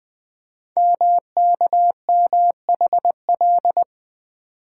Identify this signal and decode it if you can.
Morse code, heard as a keyed tone, about 20 wpm, 700 Hz. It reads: MKMHL
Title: Code Groups - 5 Letters - 20wpm